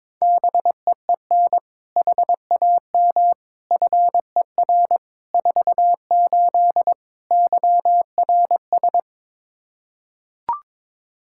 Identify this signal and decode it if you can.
Morse code, heard as a keyed tone, about 21 wpm, 700 Hz. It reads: BEEN HAM FER 48 YRS
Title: Practice QSO - Exchange 3 - More Info - 22wpm